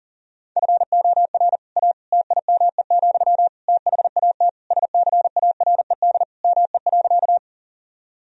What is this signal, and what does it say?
Morse code, heard as a keyed tone, about 40 wpm, 700 Hz. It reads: FOR A TIME, THAT SCARED ME.
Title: Sentences from Top 1000 Words - Review Part 4 - 40wpm